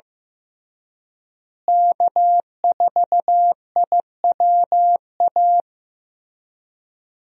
Morse code, heard as a keyed tone, about 15 wpm, 700 Hz. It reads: K4IWA
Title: USA Call Signs - 5 Characters - Repeated 3x - 15wpm